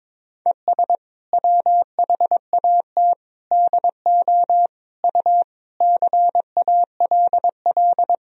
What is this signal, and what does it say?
Morse code, heard as a keyed tone, about 22 wpm, 700 Hz. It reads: ES WHAT DO U CALL
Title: Sentences from Top 100 Words and Abbreviations 22wpm